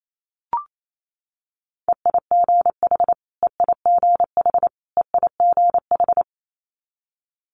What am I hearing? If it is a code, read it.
Morse code, heard as a keyed tone, about 28 wpm, 700 Hz. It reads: EIGH EIGH EIGH
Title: Multi-letter Phonemes Repeated 3x 28wpm